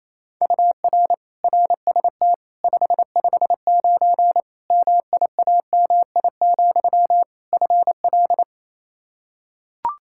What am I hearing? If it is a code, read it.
Morse code, heard as a keyed tone, about 27 wpm, 700 Hz. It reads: UR RST 559 MIAMI, FL
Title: K1USN SST QSO Practice 28wpm